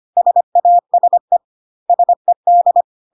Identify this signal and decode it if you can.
Morse code, heard as a keyed tone, about 25 wpm, 700 Hz. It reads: SASE SED
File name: CW abbreviations alphabetical repeated 3 times 25 25wpm ik1hge